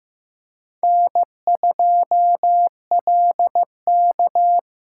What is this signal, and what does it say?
Morse code, heard as a keyed tone, about 15 wpm, 700 Hz. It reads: N2LK
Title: K1USN SST QSO Practice - 15wpm